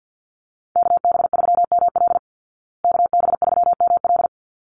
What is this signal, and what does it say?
Morse code, heard as a keyed tone, about 50 wpm, 700 Hz. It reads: KB3ML KB3ML
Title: Call Signs Repeated 3x - 50wpm